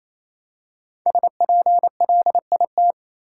Morse code, heard as a keyed tone, about 28 wpm, 700 Hz. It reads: SPLIT
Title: Five Letter Words - 28wpm